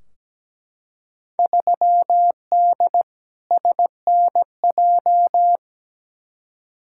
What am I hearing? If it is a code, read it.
Morse code, heard as a keyed tone, about 17 wpm, 700 Hz. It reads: 3D SNJ
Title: Field Day Exchanges Encore - 17wpm